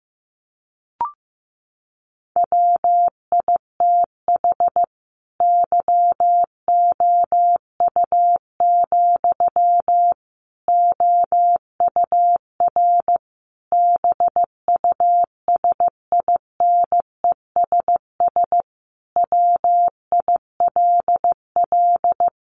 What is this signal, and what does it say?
Morse code, heard as a keyed tone, about 15 wpm, 700 Hz. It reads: WITH YOU, OUR BUSINESS WILL
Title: Sentences from Top 1400 Words 15wpm